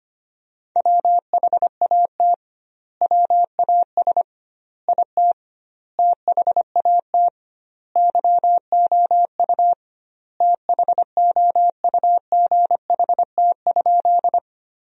Morse code, heard as a keyed tone, about 25 wpm, 700 Hz. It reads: WHAT WAS IT THAT YOU THOUGHT?
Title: 2x Word-Spacing - Sentences from Top 300 Words - 25wpm